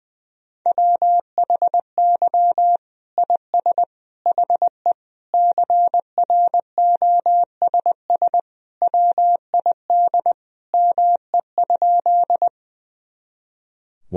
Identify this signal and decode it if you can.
Morse code, heard as a keyed tone, about 19 wpm, 700 Hz. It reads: WHY IS HE CROSS WID ME?
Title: Sentences from Top 300 Words and Abbreviations 20wpm